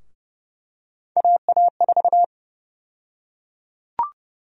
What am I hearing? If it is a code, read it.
Morse code, heard as a keyed tone, about 31 wpm, 700 Hz. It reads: AA4
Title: USA Call Sign Prefixes - Repeated 2x - 30wpm